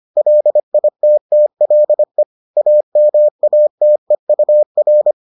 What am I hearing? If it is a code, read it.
Morse code, heard as a keyed tone, about 25 wpm, 600 Hz. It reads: LITTLE AMATEUR